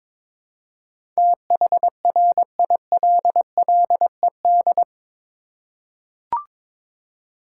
Morse code, heard as a keyed tone, about 22 wpm, 700 Hz. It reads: THRILLED
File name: practice / 22wpm / Eight Letter Words - 22wpm